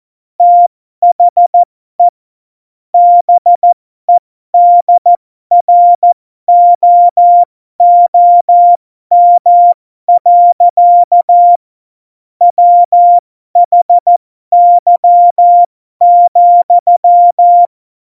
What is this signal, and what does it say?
Morse code, heard as a keyed tone, about 14 wpm, spaced as for 12 wpm, 700 Hz. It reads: THE BEDROOM. WHY,
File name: Morsekurs / Witze